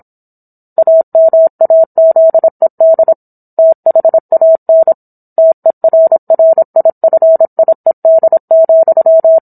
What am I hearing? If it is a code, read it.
Morse code, heard as a keyed tone, about 27 wpm, 650 Hz. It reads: AMAZED THAN TERRIFIED,